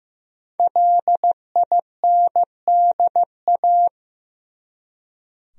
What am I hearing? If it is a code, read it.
Morse code, heard as a keyed tone, about 15 wpm, 700 Hz. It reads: LINDA
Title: USA Common Names - Max 5 Letters - 15wpm